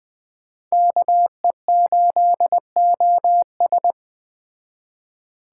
Morse code, heard as a keyed tone, about 20 wpm, 700 Hz. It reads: KE8OS
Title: Speed Racing - USA 2x2 Call Signs 17wpm